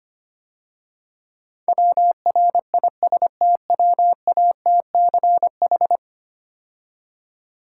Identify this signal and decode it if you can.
Morse code, heard as a keyed tone, about 29 wpm, 700 Hz. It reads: WRISTWATCH
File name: practice / 25wpm / Difficult Words - Speed-Racing with Hints - 25wpm